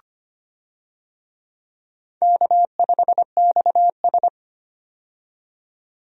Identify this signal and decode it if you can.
Morse code, heard as a keyed tone, about 25 wpm, 700 Hz. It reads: K5XS
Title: USA Call Signs - 4 Characters - 25wpm